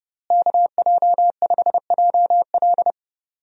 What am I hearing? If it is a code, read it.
Morse code, heard as a keyed tone, about 30 wpm, 700 Hz. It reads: KJ5JL
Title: USA Call Signs - 5 Characters - Repeated 3x - 30wpm